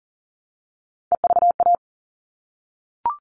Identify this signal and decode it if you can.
Morse code, heard as a keyed tone, about 40 wpm, 700 Hz. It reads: EVA
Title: All Cut Numbers - 3-Digit 40wpm